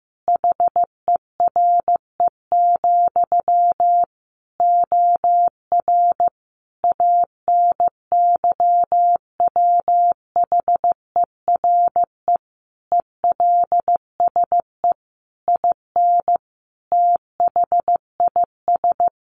Morse code, heard as a keyed tone, about 15 wpm, 700 Hz. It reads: HERE, OR ANYWHERE ELSE IN THIS
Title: Sentences from Top 1400 Words 15wpm